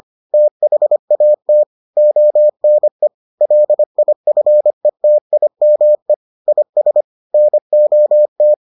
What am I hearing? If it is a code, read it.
Morse code, heard as a keyed tone, about 25 wpm, 600 Hz. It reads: THAT ONE LIFETIME IS NOT